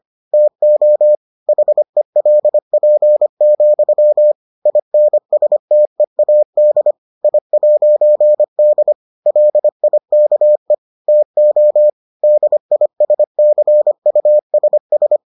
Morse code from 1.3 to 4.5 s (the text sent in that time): HELP,